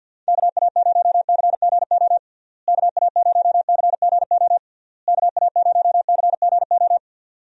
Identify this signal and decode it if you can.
Morse code, heard as a keyed tone, about 52 wpm, 700 Hz. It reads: KA0CGO KA0CGO KA0CGO
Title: USA Call Signs - 6 Characters - Repeated 3x - 50wpm